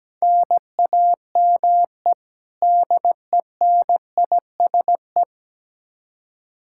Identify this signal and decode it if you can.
Morse code, heard as a keyed tone, about 17 wpm, 700 Hz. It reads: NAME DENISE
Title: Practice QSO - Exchange 1 - Beginning - 17wpm